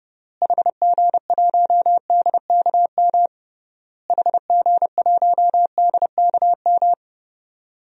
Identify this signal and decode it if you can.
Morse code, heard as a keyed tone, about 30 wpm, 700 Hz. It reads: HG1DKM HG1DKM
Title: DXCC-100 Call Signs - 6 Characters - Repeated 3x - 30wpm